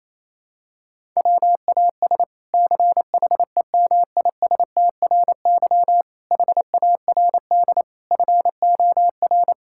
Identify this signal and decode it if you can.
Morse code, heard as a keyed tone, about 28 wpm, 700 Hz. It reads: WAS CHEMISTRY HARD FOR
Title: Sentences from Top 1000 Words - Review Part 6 - 28wpm